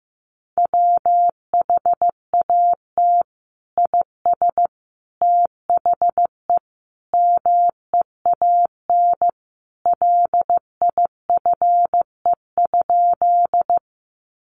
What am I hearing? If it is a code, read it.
Morse code, heard as a keyed tone, about 15 wpm, 700 Hz. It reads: WHAT IS THE MEAN LIFE?
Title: Sentences from Top 900 Words 15wpm